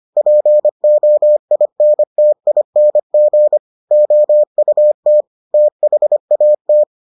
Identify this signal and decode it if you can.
Morse code, heard as a keyed tone, about 25 wpm, 600 Hz. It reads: POINTING OUT THAT